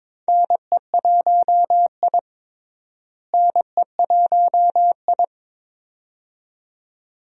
Morse code, heard as a keyed tone, about 22 wpm, 700 Hz. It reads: NE1I NE1I
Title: USA Call Signs - 4 Characters - Repeated 2x - 22wpm